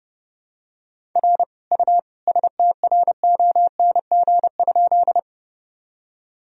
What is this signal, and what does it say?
Morse code, heard as a keyed tone, about 32 wpm, 700 Hz. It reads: R U STRONG?
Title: Sentences from Top 500 Words and Abbreviations 30wpm